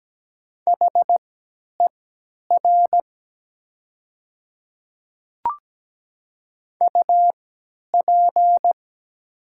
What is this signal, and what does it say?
Morse code, heard as a keyed tone, about 17 wpm, spaced as for 10 wpm, 700 Hz. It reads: HER UP
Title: 3x Character Spacing - Top 300 Words - 17wpm